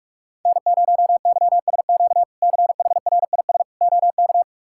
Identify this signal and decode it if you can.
Morse code, heard as a keyed tone, about 41 wpm, 700 Hz. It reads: N0YSQ CHRIS OK
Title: K1USN SST QSO Practice - 45wpm